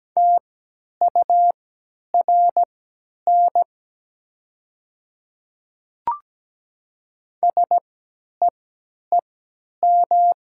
Morse code, heard as a keyed tone, about 17 wpm, spaced as for 10 wpm, 700 Hz. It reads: TURN SEEM
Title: 3x Character Spacing - Top 300 Words - 17wpm